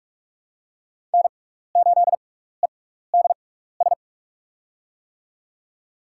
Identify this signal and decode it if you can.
Morse code, heard as a keyed tone, about 44 wpm, spaced as for 15 wpm, 700 Hz. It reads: N8EDS
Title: USA Call Signs - 6x Character Spacing - 45wpm